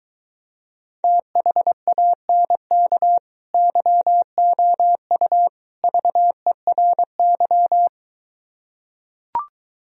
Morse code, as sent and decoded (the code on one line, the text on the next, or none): - .... .- -. -.- / -.-- --- ..- / ...- . .-. -.--
THANK YOU VERY